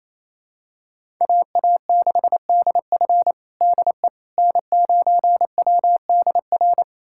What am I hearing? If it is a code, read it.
Morse code, heard as a keyed tone, about 28 wpm, 700 Hz. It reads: AA6DF DE N9WDR
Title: USA Callsign DE Callsign - 5 Characters - 28wpm